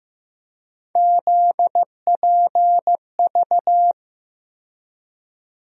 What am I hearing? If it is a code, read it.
Morse code, heard as a keyed tone, about 15 wpm, 700 Hz. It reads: ZPV
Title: USA Call Sign Postfixes - 3 Letters 15wpm